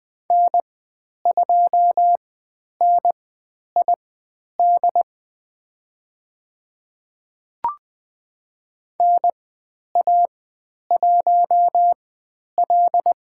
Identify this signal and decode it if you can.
Morse code, heard as a keyed tone, about 20 wpm, spaced as for 10 wpm, 700 Hz. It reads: N2NID NA1L
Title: 095 Lesson 16g - Letter c - Call Signs 20wpm